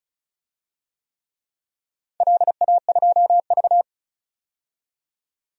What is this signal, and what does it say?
Morse code, heard as a keyed tone, about 36 wpm, 700 Hz. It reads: LA2V
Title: DXCC-100 Call Signs - 4 Characters - 35wpm